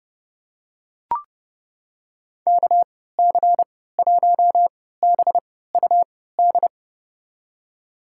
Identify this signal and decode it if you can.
Morse code, heard as a keyed tone, about 30 wpm, spaced as for 17 wpm, 700 Hz. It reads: KC1BUD
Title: USA Call Signs - 6 Characters - 3x Character Spacing - 30wpm